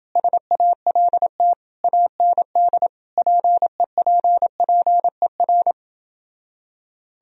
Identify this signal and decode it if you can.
Morse code, heard as a keyed tone, about 30 wpm, 700 Hz. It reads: SALT AND PEPPER